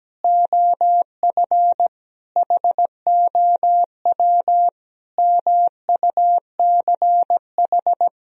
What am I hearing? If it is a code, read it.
Morse code, heard as a keyed tone, about 17 wpm, 700 Hz. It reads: OF HOW MUCH
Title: Sentences from Top 1000 Words - Review Part 3 - 17wpm